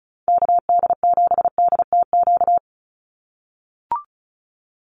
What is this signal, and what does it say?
Morse code, heard as a keyed tone, about 34 wpm, 700 Hz. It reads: KD7DTQ
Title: USA Call Signs 35wpm